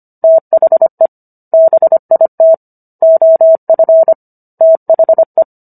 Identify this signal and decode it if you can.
Morse code, heard as a keyed tone, about 25 wpm, 650 Hz. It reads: THE BIT OF THE